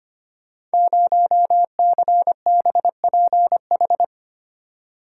Morse code, heard as a keyed tone, about 25 wpm, 700 Hz. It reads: 0CBPH
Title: Code Groups - 5 Letter:Numbers - 25wpm